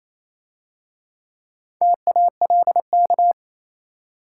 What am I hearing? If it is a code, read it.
Morse code, heard as a keyed tone, about 27 wpm, 690 Hz. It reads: TALK
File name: Easy Words - Speed-Racing with Hints - 22wpm